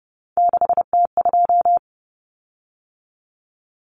Morse code, heard as a keyed tone, about 30 wpm, 700 Hz. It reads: 6T2
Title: Common Cut Numbers - 3-Digit 30wpm